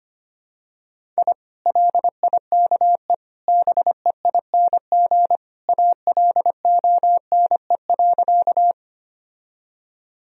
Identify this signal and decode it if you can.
Morse code, heard as a keyed tone, about 25 wpm, 700 Hz. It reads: I LIKE BEING ALONE.
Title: Sentences from Top 1000 Words - Review Part 3 - 25wpm